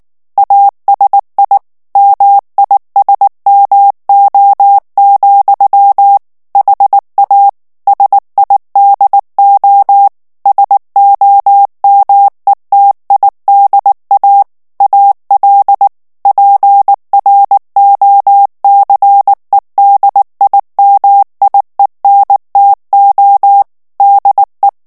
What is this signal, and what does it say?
Morse code, heard as a keyed tone, about 19 wpm, 800 Hz. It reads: ASI MISMO, HA SIDO SOMETIDA AL PROCEDIMIENTO DE
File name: morse mp3 / ESPAGNOL TEXTE EN CLAIR 2 VITESSE 18 MOTS MINUTES